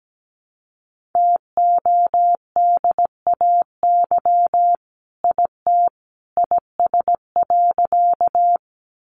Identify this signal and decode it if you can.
Morse code, heard as a keyed tone, about 17 wpm, 700 Hz. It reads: TODAY IT IS.
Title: Sentences from Top 600 Words 17wpm